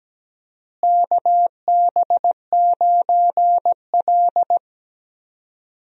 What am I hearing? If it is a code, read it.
Morse code, heard as a keyed tone, about 17 wpm, 700 Hz. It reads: KB9L
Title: USA Call Signs - 4 Characters - Repeated 2x - 17wpm